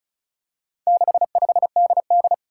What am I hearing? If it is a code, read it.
Morse code, heard as a keyed tone, about 36 wpm, 700 Hz. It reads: 65DD
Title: Four Challenging Letter-Number Combinations 35wpm